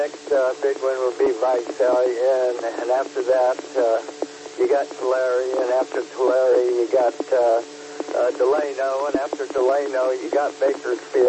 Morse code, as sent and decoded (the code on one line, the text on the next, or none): none